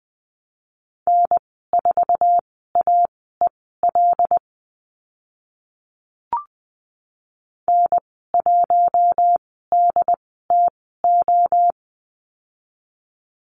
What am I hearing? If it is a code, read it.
Morse code, heard as a keyed tone, about 20 wpm, spaced as for 15 wpm, 700 Hz. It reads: N4AEL N1DTO
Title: 076 Lesson 13g - Letter L - Call Signs 20wpm